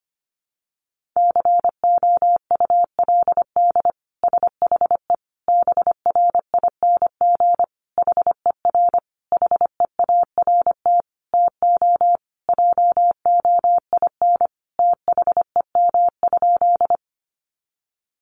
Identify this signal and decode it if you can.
Morse code, heard as a keyed tone, about 25 wpm, 700 Hz. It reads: COULD SHE BRING HER HEART TO JOIN THEM?
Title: Sentences from Top 1400 Words 25wpm